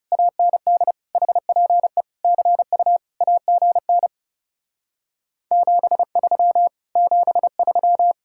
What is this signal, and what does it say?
Morse code, heard as a keyed tone, about 40 wpm, 700 Hz. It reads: AND HPE CU AGN 73 73